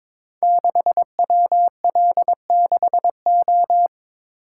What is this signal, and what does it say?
Morse code, heard as a keyed tone, about 22 wpm, 700 Hz. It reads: 6WL6O
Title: Code Groups - 5 Letter:Numbers - 22wpm